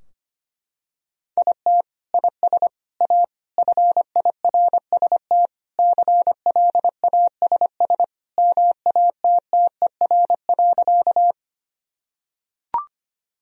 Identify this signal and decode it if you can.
Morse code, heard as a keyed tone, about 25 wpm, 700 Hz. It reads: IT IS A FIRST CLASS MATTER.
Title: Sentences from Top 1000 Words - Review Part 2 - 25wpm